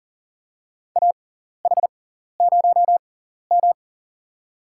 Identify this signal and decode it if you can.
Morse code, heard as a keyed tone, about 40 wpm, spaced as for 14 wpm, 700 Hz. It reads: AH0M
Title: USA Call Signs - 4 Characters - 6x Character Spacing - 40wpm